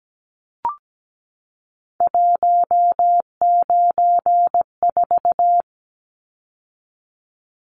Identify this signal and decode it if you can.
Morse code, heard as a keyed tone, about 17 wpm, 700 Hz. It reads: 194
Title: Numbers - 3-Digit 17wpm